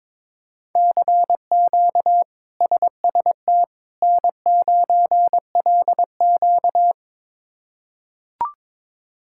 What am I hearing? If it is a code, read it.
Morse code, heard as a keyed tone, about 22 wpm, 700 Hz. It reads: CQ SST N9LQ
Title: K1USN SST QSO Practice - 22wpm